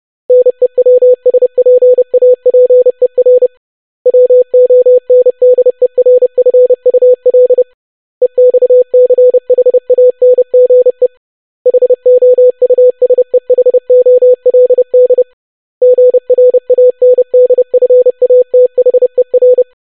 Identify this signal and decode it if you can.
Morse code, heard as a keyed tone, about 30 wpm, 500 Hz. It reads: NEWSPAPER WONDERFUL EXCHANGE HOUSEHOLD GRANDFATHER
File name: longwords-25wpm